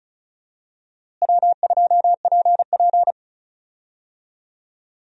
Morse code, heard as a keyed tone, about 34 wpm, 700 Hz. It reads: W2PP
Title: USA Call Signs - 4 Characters - Repeated 3x - 35wpm